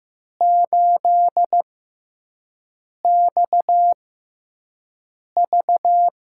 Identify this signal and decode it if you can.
Morse code, heard as a keyed tone, about 15 wpm, spaced as for 5 wpm, 700 Hz. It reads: 8XV
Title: USA Call Signs - 5 Characters - 6x Character Spacing - 15wpm